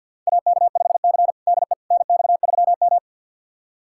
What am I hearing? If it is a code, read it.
Morse code, heard as a keyed tone, about 52 wpm, 700 Hz. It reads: AK5C DE NX3M
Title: USA Callsign DE Callsign - 4 Characters - 50wpm